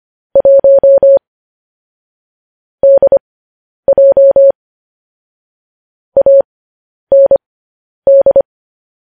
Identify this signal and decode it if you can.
Morse code, heard as a keyed tone, about 25 wpm, spaced as for 10 wpm, 550 Hz. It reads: 1 DJ AND
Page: news-headlines-in-morse-code-at-10-wpm